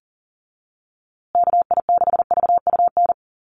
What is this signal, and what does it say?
Morse code, heard as a keyed tone, about 40 wpm, 700 Hz. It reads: KI6VUN
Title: USA Call Signs Repeated 2x 40wpm